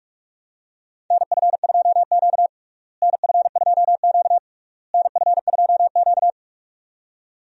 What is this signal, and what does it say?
Morse code, heard as a keyed tone, about 47 wpm, 700 Hz. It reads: NF2Q NF2Q NF2Q
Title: USA Call Signs - 4 Characters - Repeated 3x - 45wpm